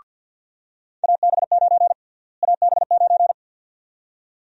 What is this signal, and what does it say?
Morse code, heard as a keyed tone, about 49 wpm, 700 Hz. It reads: AB9 AB9
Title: USA Call Sign Prefixes - Repeated 2x - 50wpm